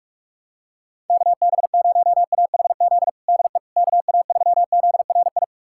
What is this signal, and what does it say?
Morse code, heard as a keyed tone, about 42 wpm, 700 Hz. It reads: KB0AHZ DE KA3ZRI
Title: USA Callsign DE Callsign - 6 Characters - 45wpm